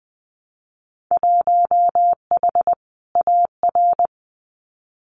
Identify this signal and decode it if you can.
Morse code, heard as a keyed tone, about 19 wpm, 700 Hz. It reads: 1H AR
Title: Winter Field Day Exchanges 20wpm